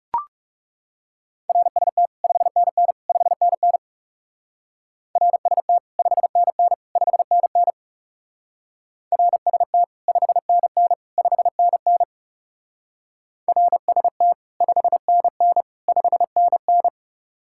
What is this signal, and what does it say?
Morse code, heard as a keyed tone, about 36 wpm, 700 Hz. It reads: RST 5NN 5NN RST 5NN 5NN RST 5NN 5NN RST 5NN 5NN
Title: Speed-Racing Common QSO Phrases Encore - 30wpm